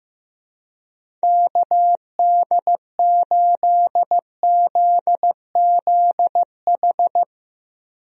Morse code, heard as a keyed tone, about 15 wpm, 700 Hz. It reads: KD8ZZH
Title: USA Call Signs - 6 Characters - 15wpm